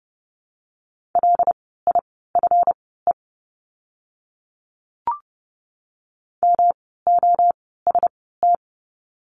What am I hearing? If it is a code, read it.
Morse code, heard as a keyed tone, about 30 wpm, spaced as for 17 wpm, 700 Hz. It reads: LIFE MOST
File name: course / 30:10wpm / 237 Lesson 38f - Letter q - Words 30wpm